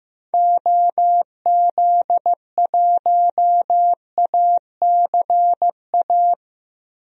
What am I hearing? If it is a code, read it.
Morse code, heard as a keyed tone, about 15 wpm, 700 Hz. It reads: OZ1ACA
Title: DXCC-100 Call Signs - 6 Characters - Repeated 3x - 15wpm